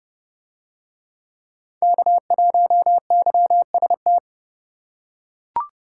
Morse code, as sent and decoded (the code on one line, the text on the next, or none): -.- .---- -.-- ... -
K1YST